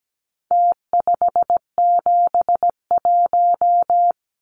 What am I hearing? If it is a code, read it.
Morse code, heard as a keyed tone, about 17 wpm, 700 Hz. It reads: T571
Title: Numbers - Contest Style - 4-Digit 17wpm